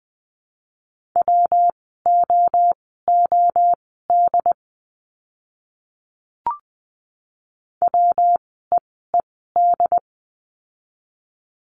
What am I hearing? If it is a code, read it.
Morse code, heard as a keyed tone, about 20 wpm, spaced as for 14 wpm, 700 Hz. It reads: WOOD WEED
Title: 115 Lesson 19f - Letter w - Words 20wpm